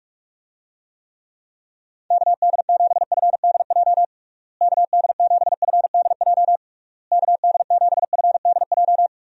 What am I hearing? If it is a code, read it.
Morse code, heard as a keyed tone, about 47 wpm, 700 Hz. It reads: KD7FDJ KD7FDJ KD7FDJ
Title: USA Call Signs - 6 Characters - Repeated 3x - 45wpm